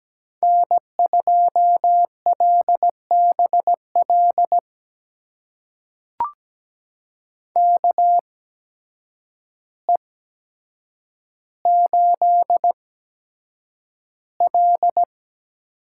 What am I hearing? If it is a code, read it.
Morse code, heard as a keyed tone, about 17 wpm, spaced as for 8 wpm, 700 Hz. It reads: N2LBL KE8L
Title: USA Call Signs - 5 Characters - 8x Character Spacing - 17wpm